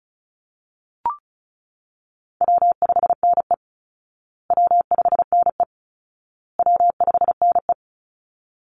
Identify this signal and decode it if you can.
Morse code, heard as a keyed tone, about 34 wpm, 700 Hz. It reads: W5NE W5NE W5NE
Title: Call Signs Repeated 2x - 35wpm